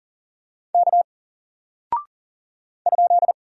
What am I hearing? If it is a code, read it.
Morse code, heard as a keyed tone, about 39 wpm, 700 Hz. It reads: K ?
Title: Single Character (Letter:Number:Punctuation:Prosign) 40wpm